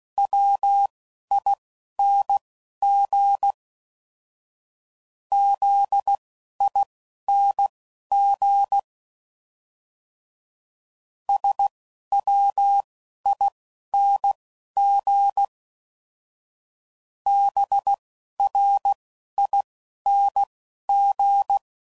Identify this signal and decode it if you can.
Morse code, heard as a keyed tone, about 16 wpm, spaced as for 11 wpm, 800 Hz. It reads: WING ZING SWING BRING
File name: Suffixes / Special INGs / Single INGs